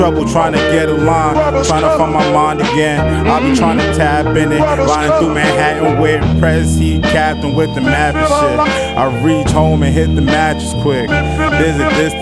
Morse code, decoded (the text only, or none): none